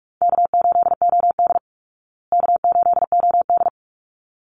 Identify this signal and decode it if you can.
Morse code, heard as a keyed tone, about 44 wpm, 700 Hz. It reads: K8OD K8OD
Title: Easy Call Signs Repeated 2x - 45wpm